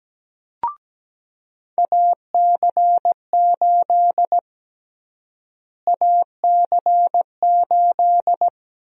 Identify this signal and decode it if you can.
Morse code, heard as a keyed tone, about 17 wpm, 700 Hz. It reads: AC8 AC8
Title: USA Call Sign Prefixes - Repeated 2x - 17wpm